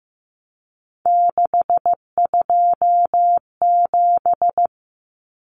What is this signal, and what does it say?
Morse code, heard as a keyed tone, about 15 wpm, 700 Hz. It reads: 627
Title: Numbers - 3-Digit 15wpm